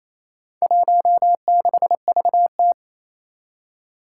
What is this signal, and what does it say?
Morse code, heard as a keyed tone, about 27 wpm, 700 Hz. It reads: 16VT
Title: Four Challenging Letter-Number Combinations 28wpm